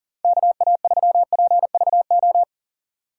Morse code, heard as a keyed tone, about 37 wpm, 700 Hz. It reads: KA3PVO
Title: USA Call Signs - 6 Characters - 4x Character Spacing - 40wpm